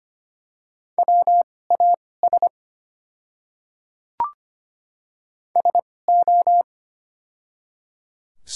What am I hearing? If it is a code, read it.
Morse code, heard as a keyed tone, about 25 wpm, spaced as for 18 wpm, 700 Hz. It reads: WAS SO
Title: 2x Character Spacing - Top 100 Words - 25wpm